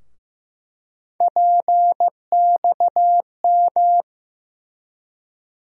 Morse code, heard as a keyed tone, about 15 wpm, 700 Hz. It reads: PXM
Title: USA Call Sign Postfixes - 3 Letters 15wpm